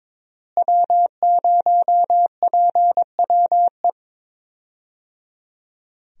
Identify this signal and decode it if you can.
Morse code, heard as a keyed tone, about 22 wpm, 700 Hz. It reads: W0PWE
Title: USA Call Signs - 5 Characters - Repeated 3x - 22wpm